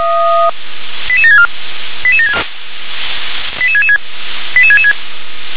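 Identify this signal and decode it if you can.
Five-tone selective call (ZVEI), heard as quick runs of five tones, tones of about 70 ms. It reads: 88764 88666 88666
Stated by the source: ZVEI five-tone selective call received on SDR.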